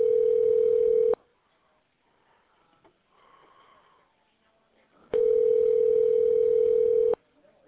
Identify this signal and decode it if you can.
Ringback tone (North America), 440 + 480 Hz.